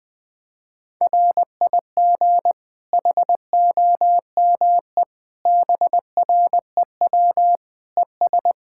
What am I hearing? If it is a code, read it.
Morse code, heard as a keyed tone, about 21 wpm, 700 Hz. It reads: RIG HOME BREW ES
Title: Practice QSO - Exchange 2 - Info - 20wpm